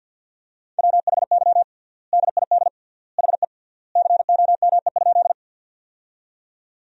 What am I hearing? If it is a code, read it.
Morse code, heard as a keyed tone, about 52 wpm, 700 Hz. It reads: WHY DID HE COME?
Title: 3x Word-Spacing - Sentences from Top 200 Words - 50wpm